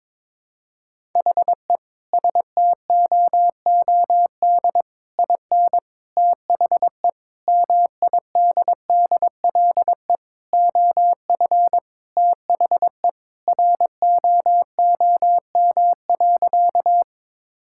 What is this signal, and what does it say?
Morse code, heard as a keyed tone, about 22 wpm, 700 Hz. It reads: HE STOOD IN THE MIDDLE OF THE ROOM.
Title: Sentences from Top 1000 Words - Review Part 6 - 22wpm